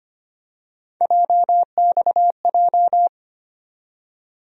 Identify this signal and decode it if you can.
Morse code, heard as a keyed tone, about 25 wpm, 700 Hz. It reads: JXJ
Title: USA Call Sign Postfixes - 3 Letters 25wpm